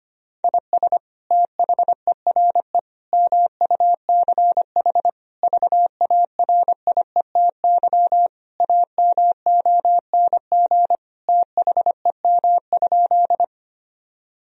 Morse code, heard as a keyed tone, about 25 wpm, 700 Hz. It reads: IS THERE MUCH VARIETY AMONG THEM?
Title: Sentences from Top 1000 Words - Review Part 4 - 25wpm